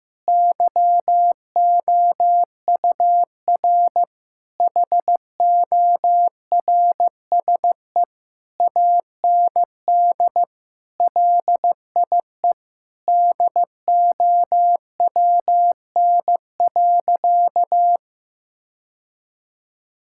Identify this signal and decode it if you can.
Morse code, heard as a keyed tone, about 15 wpm, 700 Hz. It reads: YOUR HORSE AND LIE DOWN.
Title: Sentences from Top 1600 Words 15wpm